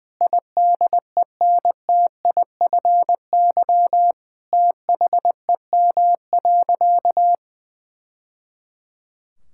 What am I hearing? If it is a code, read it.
Morse code, heard as a keyed tone, about 19 wpm, 700 Hz. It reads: IDENTIFY THEM.
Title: Sentences from Top 1000 Words - Review Part 6 - 20wpm